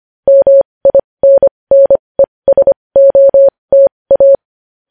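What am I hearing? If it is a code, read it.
Morse code, heard as a keyed tone, about 25 wpm, spaced as for 20 wpm, 550 Hz. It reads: MINNESOTA
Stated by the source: Morse news headlines sent at 20.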